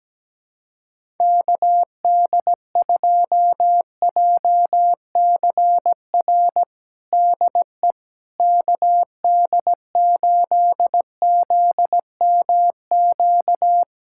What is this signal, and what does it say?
Morse code, heard as a keyed tone, about 17 wpm, 700 Hz. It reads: KD2JCR DE KD8ZMQ
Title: USA Callsign DE Callsign 17wpm